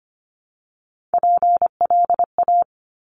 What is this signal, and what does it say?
Morse code, heard as a keyed tone, about 25 wpm, 700 Hz. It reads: PLA
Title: Top Three Letter Combinations in Words 25wpm